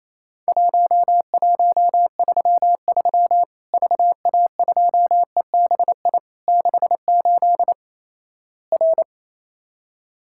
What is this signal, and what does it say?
Morse code, heard as a keyed tone, about 28 wpm, 700 Hz. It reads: 1133 VA2EBI 68 R
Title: Worked All Europe DX Contest - QTC Exchange - 28wpm